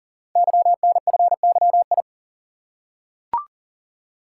Morse code, heard as a keyed tone, about 38 wpm, 690 Hz. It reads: YNFYI
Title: Code Groups - 5 Letters - 40wpm